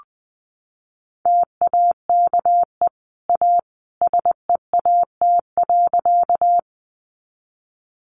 Morse code, heard as a keyed tone, about 20 wpm, 700 Hz. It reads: TAKE A SEAT.
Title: Sentences from Top 1500 Words 20wpm